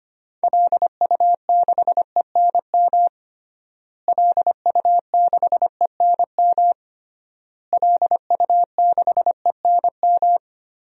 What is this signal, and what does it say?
Morse code, heard as a keyed tone, about 25 wpm, 700 Hz. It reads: LU6ENM LU6ENM LU6ENM
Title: DXCC-100 Call Signs - 6 Characters - Repeated 3x - 25wpm